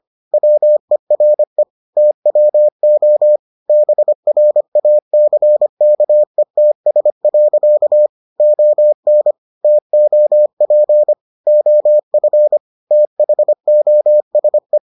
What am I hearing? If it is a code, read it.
Morse code, heard as a keyed tone, about 25 wpm, 600 Hz. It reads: WERE TWO BRACKETS. ON TOP OF THOSE